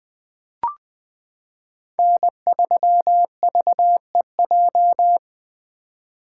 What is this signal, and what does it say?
Morse code, heard as a keyed tone, about 20 wpm, 700 Hz. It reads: N3VEJ
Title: 2023 POTA Activator Call Signs - 20wpm